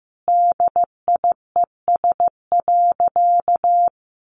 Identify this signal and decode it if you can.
Morse code, heard as a keyed tone, about 15 wpm, 700 Hz. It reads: DIES.
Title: Additional Easy Phrases 15wpm